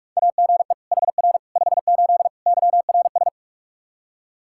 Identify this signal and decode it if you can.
Morse code, heard as a keyed tone, about 46 wpm, 700 Hz. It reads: AGE HR 58 YRS
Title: Practice QSO - Exchange 3 - More Info - 45wpm